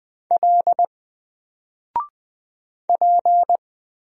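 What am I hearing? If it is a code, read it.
Morse code, heard as a keyed tone, about 20 wpm, 700 Hz. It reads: L P
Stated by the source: Single letter and number Morse drill at 20 wpm, from N4LCD.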